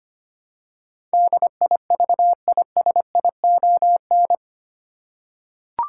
Morse code, heard as a keyed tone, about 25 wpm, 700 Hz. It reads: DIVISION
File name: Eight Letter Words - 25wpm